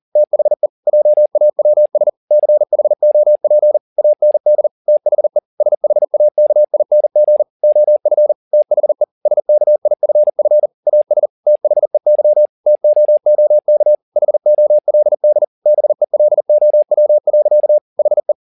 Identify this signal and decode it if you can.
Morse code, heard as a keyed tone, about 40 wpm, 600 Hz. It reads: THE JAWS CHOP AND THE SHAKING OF THE SKIFF AS THEY TOOK HOLD BELOW. HE
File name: the old man and the sea 2 40wpm